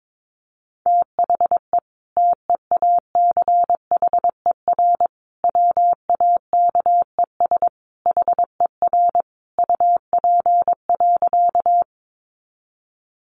Morse code, heard as a keyed tone, about 22 wpm, 700 Hz. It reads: THE TEACHER WAKES HER UP.
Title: Easy Phrases 22wpm